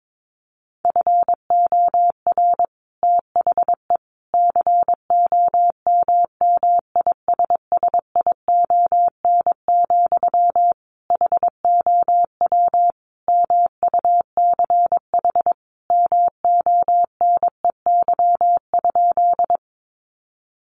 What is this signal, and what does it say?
Morse code, heard as a keyed tone, about 22 wpm, 700 Hz. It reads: FOR THE COMMISSION, HOW MUCH MONEY?
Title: Sentences from Top 1400 Words 22wpm